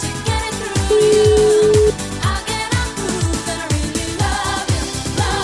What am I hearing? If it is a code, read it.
Ringback tone (Europe, 425 Hz), with noise.